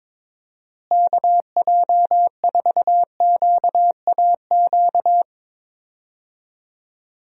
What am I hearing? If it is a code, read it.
Morse code, heard as a keyed tone, about 22 wpm, 700 Hz. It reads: KJ4QAQ
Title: USA Call Signs - 6 Characters - Repeated 2x - 22wpm